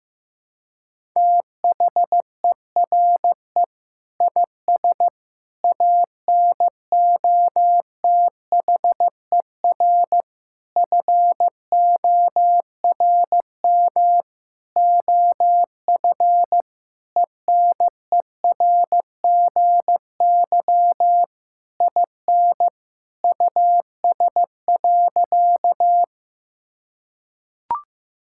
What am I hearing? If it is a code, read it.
Morse code, heard as a keyed tone, about 16 wpm, 700 Hz. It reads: THERE IS ANOTHER FORM OF ENERGY IN US.
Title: Sentences from Top 1000 Words - Review Part 6 - 15wpm